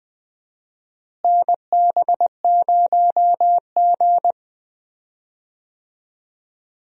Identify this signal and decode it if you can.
Morse code, heard as a keyed tone, about 20 wpm, 700 Hz. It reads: NB0G
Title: USA Call Signs - 4 Characters - Repeated 3x - 20wpm